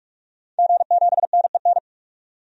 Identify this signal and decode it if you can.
Morse code, heard as a keyed tone, about 42 wpm, 700 Hz. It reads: G7NEN
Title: DXCC-100 Call Signs - 5 Characters - 45wpm